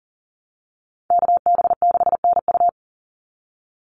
Morse code, heard as a keyed tone, about 40 wpm, 700 Hz. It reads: KB6NU
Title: Call Signs - 40wpm